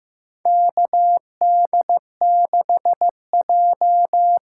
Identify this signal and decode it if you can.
Morse code, heard as a keyed tone, about 15 wpm, 700 Hz. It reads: KD6J
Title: USA Call Signs - 6 Characters - 6x Character Spacing - 15wpm